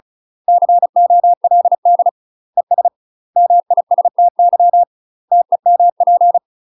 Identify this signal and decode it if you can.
Morse code, heard as a keyed tone, about 33 wpm, 700 Hz. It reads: COLD ES MISTY TEMP
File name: UK MASTER QSOs TRAINING / 35WPM / UK MASTER QSOs 0012